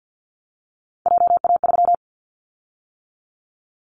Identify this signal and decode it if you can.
Morse code, heard as a keyed tone, about 50 wpm, 700 Hz. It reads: JA3
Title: Callsign Prefixes for Top 25 DXCC - 50wpm